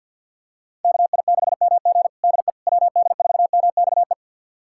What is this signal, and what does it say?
Morse code, heard as a keyed tone, about 48 wpm, 700 Hz. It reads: KI6MG DE WD4MXE
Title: USA Callsign DE Callsign 50wpm